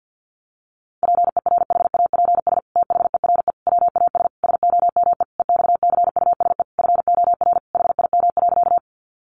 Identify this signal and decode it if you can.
Morse code, heard as a keyed tone, about 50 wpm, 700 Hz. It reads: PERHAPS THERE WAS SOME EXCUSE FOR HIM.
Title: Sentences from Top 700 Words 50wpm